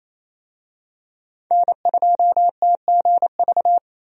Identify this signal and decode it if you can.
Morse code, heard as a keyed tone, about 27 wpm, 700 Hz. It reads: N2TGV
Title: USA Call Signs - 5 Characters - Repeated 3x - 28wpm